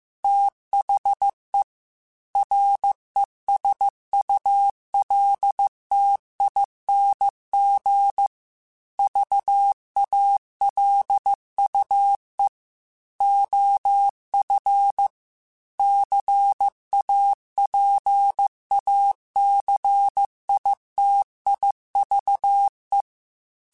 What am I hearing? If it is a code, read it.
Morse code, heard as a keyed tone, about 15 wpm, 800 Hz. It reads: THE RESULTING VALUE OF CAPACITIVE